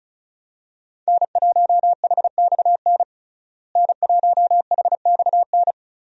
Morse code, heard as a keyed tone, about 36 wpm, 700 Hz. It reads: N1HXN N1HXN
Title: USA Call Signs - 5 Characters - Repeated 3x - 35wpm